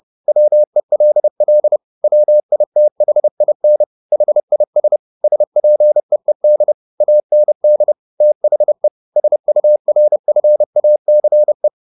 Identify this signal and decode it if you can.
Morse code, heard as a keyed tone, about 30 wpm, 600 Hz. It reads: WELL WITHIN HIS SPEED AND THE SURFACE
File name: the old man and the sea / the old man and the sea 1 30wpm